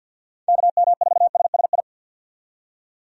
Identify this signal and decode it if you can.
Morse code, heard as a keyed tone, about 52 wpm, 700 Hz. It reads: KM4SSI
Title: USA Call Signs - 6 Characters - 50wpm